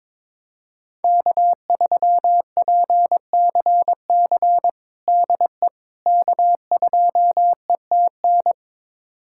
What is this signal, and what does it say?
Morse code, heard as a keyed tone, about 22 wpm, 700 Hz. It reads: K3PCC DE K2ETN
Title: USA Callsign DE Callsign - 5 Characters - 22wpm